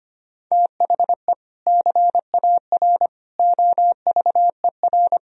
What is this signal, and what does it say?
Morse code, heard as a keyed tone, about 25 wpm, 700 Hz. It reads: THE CAR OVER
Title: Sentences from Top 1000 Words - Review Part 1 - 25wpm